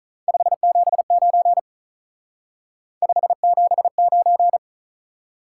Morse code, heard as a keyed tone, about 46 wpm, 700 Hz. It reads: 579 579